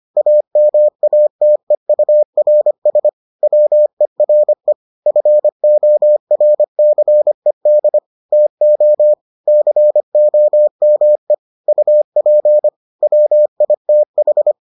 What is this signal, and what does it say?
Morse code, heard as a keyed tone, about 25 wpm, 600 Hz. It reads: AMATEURS WERE FORCED TO COME UP WITH